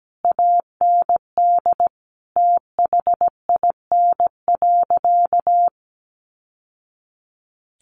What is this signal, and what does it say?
Morse code, heard as a keyed tone, about 17 wpm, 700 Hz. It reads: AND THIN.
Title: Sentences from Top 1200 Words 17wpm